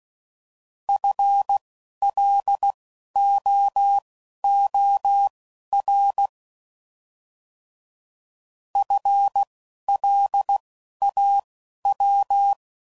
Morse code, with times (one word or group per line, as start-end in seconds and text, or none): 0.9-6.3 FLOOR
8.8-12.5 FLAW